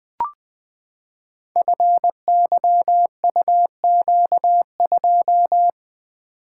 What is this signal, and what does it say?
Morse code, heard as a keyed tone, about 20 wpm, 700 Hz. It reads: FYUQ2
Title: Code Groups - 5 Letter:Numbers - 20wpm